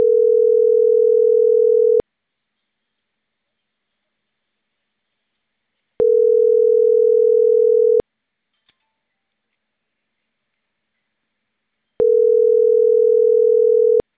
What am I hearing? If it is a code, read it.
Ringback tone (North America), 440 + 480 Hz.